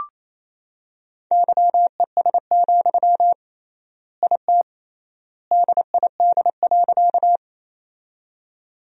Morse code, heard as a keyed tone, about 28 wpm, 700 Hz. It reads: YES, IT DID.
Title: 3x Word-Spacing - Sentences from Top 500 Words - 28wpm